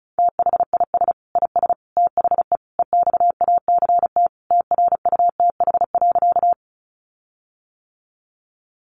Morse code, heard as a keyed tone, about 34 wpm, 700 Hz. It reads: THIS IS THE EXACT TRUTH.
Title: Sentences from Top 1300 Words 35wpm